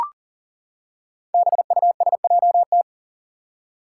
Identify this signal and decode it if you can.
Morse code, heard as a keyed tone, about 40 wpm, 700 Hz. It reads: BUSJT
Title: Five Challenging Letter-Number Combinations 40wpm